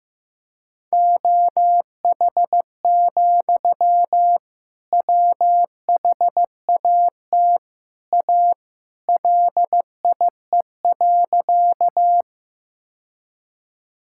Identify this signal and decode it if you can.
Morse code, heard as a keyed tone, about 15 wpm, 700 Hz. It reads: OH, WHAT A LIE.
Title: Sentences from Top 1600 Words 15wpm